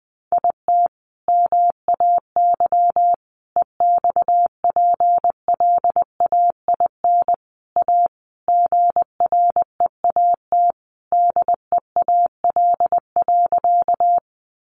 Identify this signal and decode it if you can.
Morse code, heard as a keyed tone, about 20 wpm, 700 Hz. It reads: IT MAY EXPLAIN A GREAT DEAL.
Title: Sentences from Top 1100 Words 20wpm